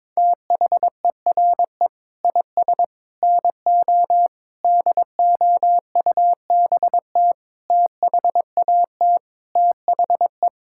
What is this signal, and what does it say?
Morse code, heard as a keyed tone, about 22 wpm, 700 Hz. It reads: THERE IS NO DOUBT THAT THE
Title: Sentences from Top 1600 Words 22wpm